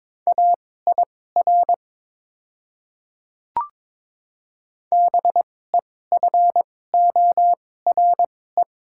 Morse code, heard as a keyed tone, about 22 wpm, spaced as for 16 wpm, 700 Hz. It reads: AIR BEFORE
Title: 2x Character Spacing - Top 300 Words - 22wpm